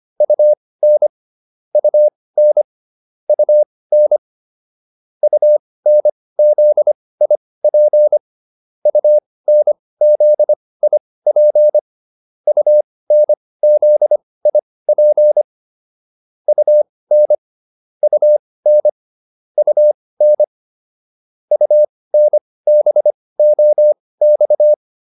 Morse code, heard as a keyed tone, about 25 wpm, spaced as for 18 wpm, 600 Hz. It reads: UN UN UN UNZIP UNZIP UNZIP UN UN UN UNBOX